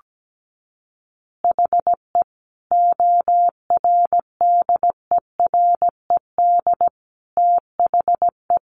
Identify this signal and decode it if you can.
Morse code, heard as a keyed tone, about 17 wpm, 700 Hz. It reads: HE ORDERED THE
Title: Additional Easy Phrases 17wpm